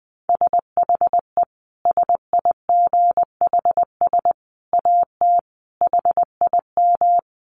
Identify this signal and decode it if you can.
Morse code, heard as a keyed tone, about 20 wpm, 700 Hz. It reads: SHE SIGHS AT HIM
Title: Dit Challenge 20wpm